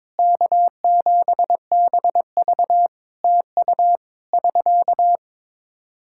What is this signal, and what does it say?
Morse code, heard as a keyed tone, about 22 wpm, 700 Hz. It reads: K7BV TU <SK>
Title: Practice QSO - Exchange 4 - Ending - 22wpm